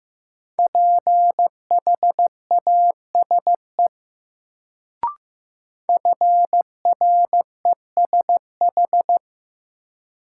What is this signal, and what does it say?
Morse code, heard as a keyed tone, about 15 wpm, 700 Hz. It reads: PHASE FRESH
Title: Sentences from Top 1200 Words - New Word Drill Alt - 15wpm